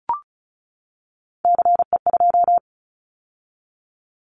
Morse code, heard as a keyed tone, about 34 wpm, 700 Hz. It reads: CE2
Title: Callsign Prefixes for Top 100 DXCC - 35wpm